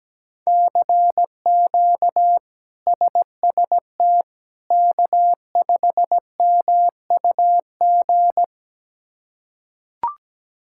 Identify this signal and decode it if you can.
Morse code, heard as a keyed tone, about 17 wpm, 700 Hz. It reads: CQ SST K5MUG
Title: K1USN SST QSO Practice - 17wpm